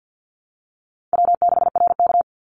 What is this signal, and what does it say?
Morse code, heard as a keyed tone, about 48 wpm, 700 Hz. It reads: W6RK
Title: Call Signs Repeated 2x - 50wpm